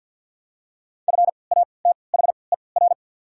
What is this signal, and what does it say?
Morse code, heard as a keyed tone, about 54 wpm, spaced as for 29 wpm, 700 Hz. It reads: FATHER
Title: 3x Character Spacing - Top 200 Words - 50wpm